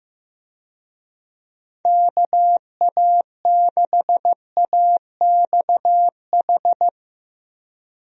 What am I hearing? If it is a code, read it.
Morse code, heard as a keyed tone, about 15 wpm, 700 Hz. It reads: KA6AXH
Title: USA Call Signs - 6 Characters - Repeated 3x - 15wpm